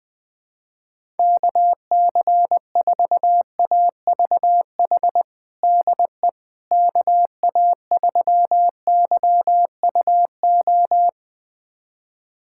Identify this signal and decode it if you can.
Morse code, heard as a keyed tone, about 19 wpm, 700 Hz. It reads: KC4AVH DE KA3YUO
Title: USA Callsign DE Callsign - 6 Characters - 20wpm